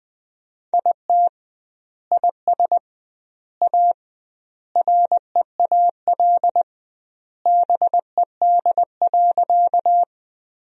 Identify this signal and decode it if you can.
Morse code, heard as a keyed tone, about 20 wpm, 700 Hz. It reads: IT IS A REAL BED.
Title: 2x Word-Spacing - Sentences from Top 500 Words - 20wpm